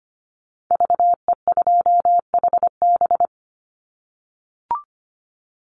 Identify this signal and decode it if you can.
Morse code, heard as a keyed tone, about 25 wpm, 700 Hz. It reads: VE2HB
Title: Call Signs - 25wpm